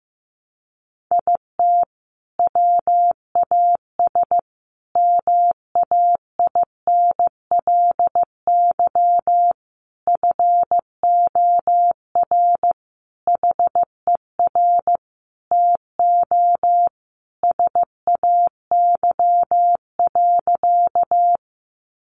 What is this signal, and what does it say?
Morse code, heard as a keyed tone, about 15 wpm, 700 Hz. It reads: IT WAS MAINLY FOR HER TO SAY.
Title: Sentences from Top 1400 Words 15wpm